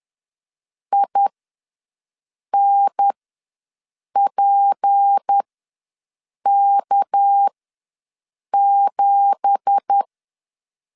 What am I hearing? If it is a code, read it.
Morse code, heard as a keyed tone, about 10 wpm, spaced as for 6 wpm, 800 Hz. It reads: INPK7
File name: LEZIONI / 10 SIGNS / 10C DOT ws 10wpm 4213